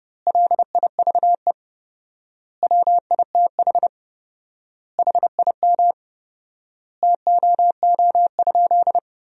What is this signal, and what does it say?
Morse code, heard as a keyed tone, about 30 wpm, 700 Hz. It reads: LIVE WITH HIM TOO?
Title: 4x Word-Spacing - Sentences from Top 500 Words - 30wpm